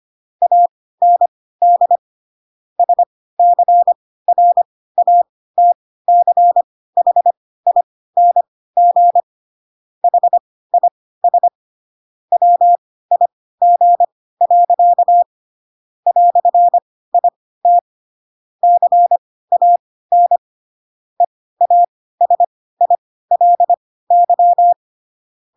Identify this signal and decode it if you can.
Morse code, heard as a keyed tone, about 25 wpm, spaced as for 16 wpm, 700 Hz. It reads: AND SCRATCHING HIS WIG. "IT CAN EASILY
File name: Pinocchio / Pinocchio ch1 10 25 ik1hge